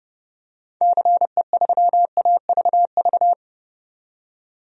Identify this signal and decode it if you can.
Morse code, heard as a keyed tone, about 32 wpm, 700 Hz. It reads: CE3AVV
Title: DXCC-100 Call Signs - 6 Characters - 30wpm